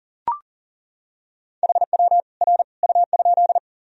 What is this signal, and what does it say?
Morse code, heard as a keyed tone, about 39 wpm, 700 Hz. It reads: HW R U?
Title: Sentences from Top 100 Words and Abbreviations 40wpm